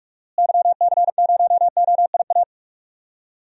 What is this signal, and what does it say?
Morse code, heard as a keyed tone, about 48 wpm, 700 Hz. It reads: YC0OIA
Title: DXCC-100 Call Signs - 6 Characters - Repeated 2x - 45wpm